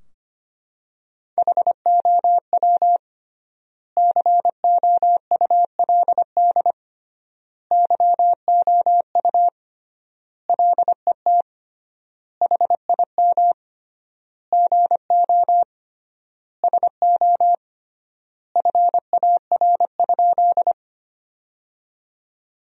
Morse code, heard as a keyed tone, about 25 wpm, 700 Hz. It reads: HOW COULD YOU LET HIM GO SO FAR?
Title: 3x Word-Spacing - Sentences from Top 300 Words - 25wpm